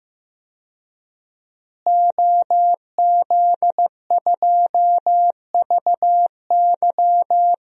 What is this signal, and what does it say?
Morse code, heard as a keyed tone, about 15 wpm, 700 Hz. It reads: OZ2VY
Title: DXCC-100 Call Signs - 5 Characters - Repeated 3x - 15wpm